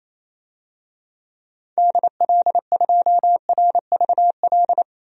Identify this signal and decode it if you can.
Morse code, heard as a keyed tone, about 28 wpm, 700 Hz. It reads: DL2RVL
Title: DXCC-100 Call Signs - 6 Characters - 28wpm